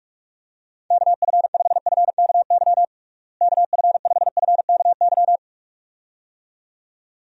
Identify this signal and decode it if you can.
Morse code, heard as a keyed tone, about 46 wpm, 700 Hz. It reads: KF5FKY KF5FKY
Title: USA Call Signs - 6 Characters - Repeated 2x - 45wpm